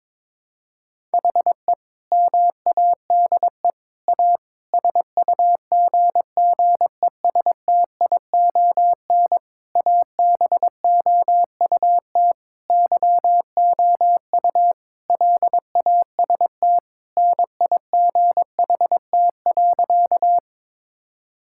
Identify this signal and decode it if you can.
Morse code, heard as a keyed tone, about 22 wpm, 700 Hz. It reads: HE MADE A SUGGESTION ABOUT YOU LAST NIGHT.
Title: Sentences from Top 1600 Words 22wpm